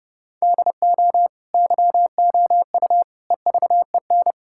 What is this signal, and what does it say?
Morse code, heard as a keyed tone, about 30 wpm, 700 Hz. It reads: DO YOU EVEN
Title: Sentences from Top 1000 Words - Review Part 5 - 30wpm